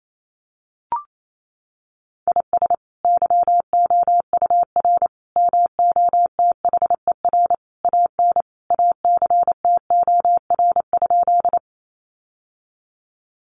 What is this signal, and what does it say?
Morse code, heard as a keyed tone, about 28 wpm, 700 Hz. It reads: IS YOUR MOTHER AN ACTOR?
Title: Sentences from Top 1500 Words 28wpm